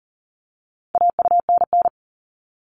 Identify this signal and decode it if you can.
Morse code, heard as a keyed tone, about 40 wpm, 700 Hz. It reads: AUNN
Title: All Cut Numbers - 4-Digit 40wpm